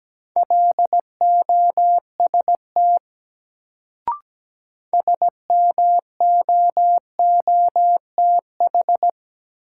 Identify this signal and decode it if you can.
Morse code, heard as a keyed tone, about 17 wpm, 700 Hz. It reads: LOST SMOOTH
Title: Sentences from Top 1400 Words - New Word Drill - 17wpm